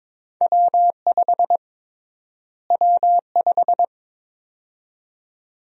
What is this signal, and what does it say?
Morse code, heard as a keyed tone, about 22 wpm, 700 Hz. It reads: W5 W5
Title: USA Call Sign Prefixes - Repeated 2x - 22wpm